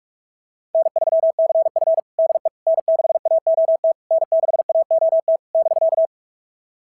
Morse code, heard as a keyed tone, about 43 wpm, 660 Hz. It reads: N3CF DE N6AOT N6AOT <BK>
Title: SOTA QSO Practice 45wpm